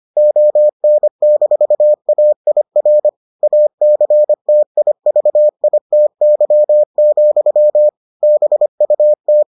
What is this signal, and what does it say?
Morse code, heard as a keyed tone, about 25 wpm, 600 Hz. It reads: ON-AIR ACTIVITY, BUT